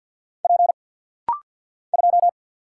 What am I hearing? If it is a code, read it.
Morse code, heard as a keyed tone, about 49 wpm, 700 Hz. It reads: P 2
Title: Single Letter-Number - Rapid-Fire 50wpm